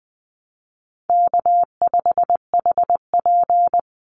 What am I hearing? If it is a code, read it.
Morse code, heard as a keyed tone, about 20 wpm, 700 Hz. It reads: K5HP
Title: Easy Call Signs Repeated 3x - 20wpm